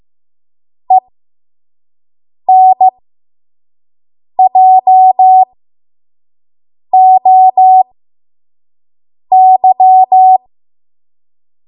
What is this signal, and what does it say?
Morse code, heard as a keyed tone, about 15 wpm, spaced as for 5 wpm, 750 Hz. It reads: ENJOY
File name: Morse / 05WPM